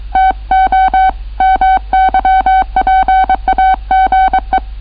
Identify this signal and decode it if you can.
Morse code, heard as a keyed tone, about 23 wpm, 760 Hz. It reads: TO MY PAGE